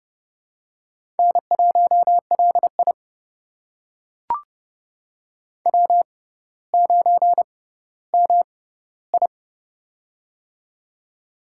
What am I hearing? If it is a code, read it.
Morse code, heard as a keyed tone, about 32 wpm, spaced as for 16 wpm, 700 Hz. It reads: N1LI W9MI
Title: USA Call Signs - 4 Characters - 6x Character Spacing - 30wpm